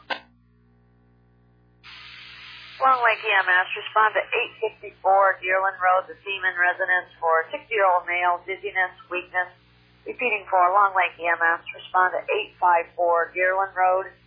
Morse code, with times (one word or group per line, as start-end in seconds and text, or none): none